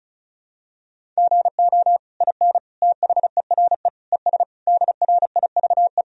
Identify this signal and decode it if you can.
Morse code, heard as a keyed tone, about 36 wpm, 700 Hz. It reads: GO IN THERE ES DRIVE
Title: Sentences from Top 500 Words and Abbreviations 35wpm